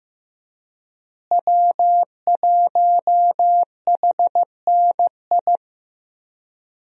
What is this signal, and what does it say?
Morse code, heard as a keyed tone, about 15 wpm, 700 Hz. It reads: W1HNI
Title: USA Call Signs - 5 Characters - Repeated 2x - 15wpm